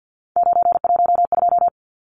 Morse code, heard as a keyed tone, about 50 wpm, 700 Hz. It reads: 912
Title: Numbers - 3-Digit 50wpm